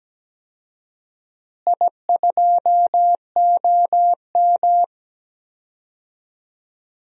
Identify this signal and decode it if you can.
Morse code, heard as a keyed tone, about 17 wpm, 700 Hz. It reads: I2OM
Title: DXCC-100 Call Signs - 4 Characters - Repeated 3x - 17wpm